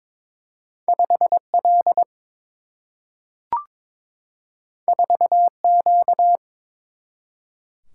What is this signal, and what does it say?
Morse code, heard as a keyed tone, about 22 wpm, 700 Hz. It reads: 5L 4Q
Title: Single-Letter Single-Number 22wpm